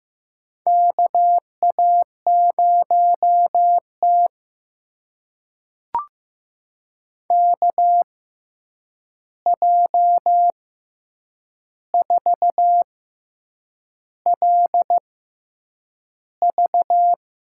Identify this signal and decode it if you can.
Morse code, heard as a keyed tone, about 15 wpm, spaced as for 7 wpm, 700 Hz. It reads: KA0T KJ4LV
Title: USA Call Signs - 6x Character Spacing - 15wpm